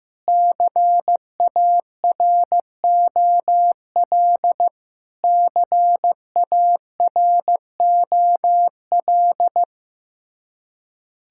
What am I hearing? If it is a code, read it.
Morse code, heard as a keyed tone, about 15 wpm, 700 Hz. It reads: CAROL CAROL
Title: Practice QSO - Exchange 1 - Beginning - 15wpm